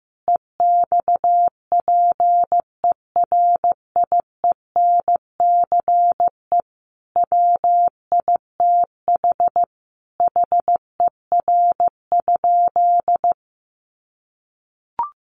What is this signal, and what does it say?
Morse code, heard as a keyed tone, about 15 wpm, 700 Hz. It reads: EXPERIENCE WITH HER?
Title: Sentences from Top 600 Words 15wpm